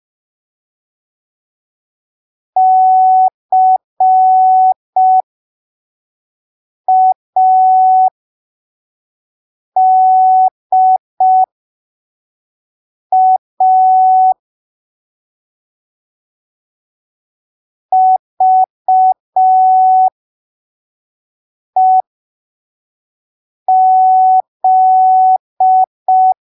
Morse code, heard as a keyed tone, about 5 wpm, spaced as for 3 wpm, 750 Hz. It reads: CADA VEZ
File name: arquivos / texto / texto 24 05ppm